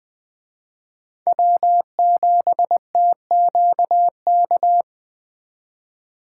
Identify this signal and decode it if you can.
Morse code, heard as a keyed tone, about 24 wpm, 700 Hz. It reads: W7TQK
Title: Speed Racing - USA 1x3 Call Signs 20wpm